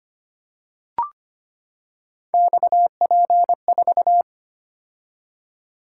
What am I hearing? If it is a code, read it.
Morse code, heard as a keyed tone, about 25 wpm, 700 Hz. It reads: XP4
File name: Three Letter-Number Combinations - 25wpm